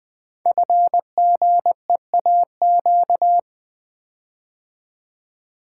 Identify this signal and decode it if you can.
Morse code, heard as a keyed tone, about 20 wpm, 700 Hz. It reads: FGEAQ
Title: Code Groups - 5 Letters - 20wpm